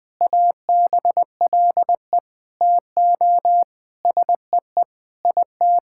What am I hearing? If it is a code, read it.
Morse code, heard as a keyed tone, about 20 wpm, 700 Hz. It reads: ABLE TO SEE IT
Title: Sentences with 4-Word Phrases - Spoken Beginning - 20wpm